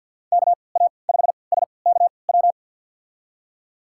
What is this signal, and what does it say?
Morse code, heard as a keyed tone, about 47 wpm, spaced as for 28 wpm, 700 Hz. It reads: KA5SKW
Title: USA Call Signs - 6 Characters - 3x Character Spacing - 50wpm